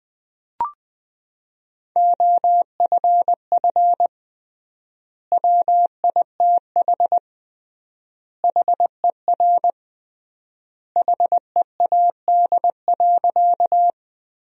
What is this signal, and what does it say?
Morse code, heard as a keyed tone, about 20 wpm, 700 Hz. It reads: OFF WITH HER HEAD.
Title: 3x Word-Spacing - Sentences from Top 300 Words - 20wpm